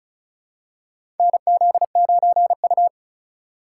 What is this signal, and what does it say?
Morse code, heard as a keyed tone, about 33 wpm, 700 Hz. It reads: NZ9U
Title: USA Call Signs - 4 Characters - 8x Character Spacing - 35wpm